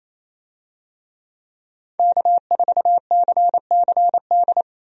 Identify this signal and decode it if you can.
Morse code, heard as a keyed tone, about 28 wpm, 700 Hz. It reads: K4CCD
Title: USA Call Signs - 5 Characters - 28wpm